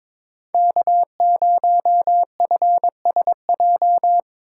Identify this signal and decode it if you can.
Morse code, heard as a keyed tone, about 25 wpm, 700 Hz. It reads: K0FSJ
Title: Speed Racing - USA 1x3 Call Signs 22wpm